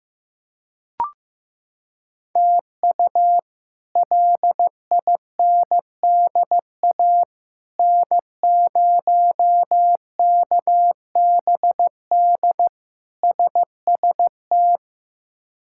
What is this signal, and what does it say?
Morse code, heard as a keyed tone, about 15 wpm, 700 Hz. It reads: TU LINDA N0KBD SST
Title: K1USN SST QSO Practice - 15wpm